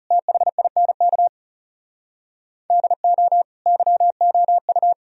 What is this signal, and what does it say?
Morse code, heard as a keyed tone, about 37 wpm, 700 Hz. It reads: THINK DO YOU
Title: Speed-Racing - Sets of 3 Words 30wpm